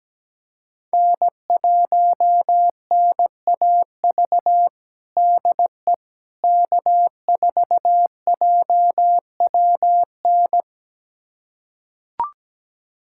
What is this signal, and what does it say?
Morse code, heard as a keyed tone, about 17 wpm, 700 Hz. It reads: N1NAV DE K4JWN
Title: USA Callsign DE Callsign - 5 Characters - 17wpm